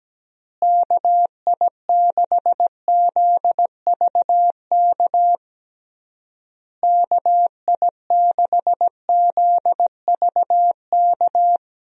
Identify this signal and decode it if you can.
Morse code, heard as a keyed tone, about 17 wpm, 700 Hz. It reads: KI6ZVK KI6ZVK
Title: USA Call Signs - 6 Characters - Repeated 3x - 17wpm